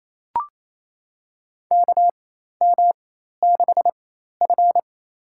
Morse code, heard as a keyed tone, about 28 wpm, spaced as for 13 wpm, 700 Hz. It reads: KM6F
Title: USA Call Signs - 4x Character Spacing - 28wpm